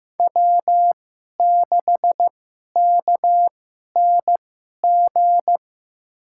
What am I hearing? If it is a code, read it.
Morse code, heard as a keyed tone, about 15 wpm, spaced as for 11 wpm, 700 Hz. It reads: W6KNG
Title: USA Call Signs - 5 Characters - 2x Character Spacing - 15wpm